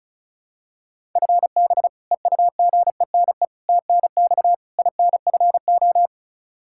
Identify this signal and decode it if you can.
Morse code, heard as a keyed tone, about 36 wpm, 700 Hz. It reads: FB EUGENE TNX INFO
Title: Practice QSO - Exchange 3 - More Info - 35wpm